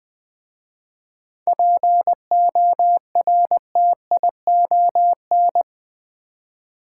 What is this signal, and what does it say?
Morse code, heard as a keyed tone, about 20 wpm, 690 Hz. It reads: PORTION
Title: 7 Letter Words - First 3 Letters Given - 20wpm